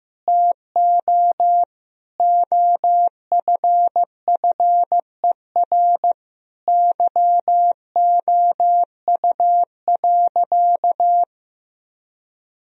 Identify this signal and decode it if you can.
Morse code, heard as a keyed tone, about 16 wpm, 700 Hz. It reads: TO OFFER YOU.
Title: Sentences from Top 1000 Words - Review Part 4 - 15wpm